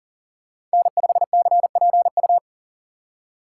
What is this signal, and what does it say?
Morse code, heard as a keyed tone, about 40 wpm, 700 Hz. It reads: N5CPU
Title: USA Call Signs - 5 Characters - 3x Character Spacing - 40wpm